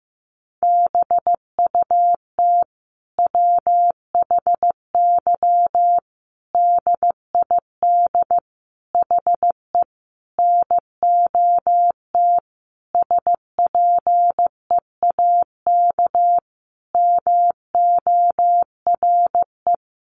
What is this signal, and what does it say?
Morse code, heard as a keyed tone, about 15 wpm, 700 Hz. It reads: BUT WHY DID HE NOT SPEAK MORE
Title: Sentences from Top 1300 Words 15wpm